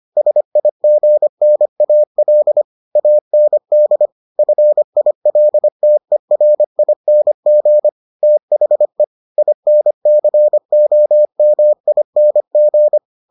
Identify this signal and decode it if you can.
Morse code, heard as a keyed tone, about 25 wpm, 600 Hz. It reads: SIGNAL AND FILTERING THE INCOMING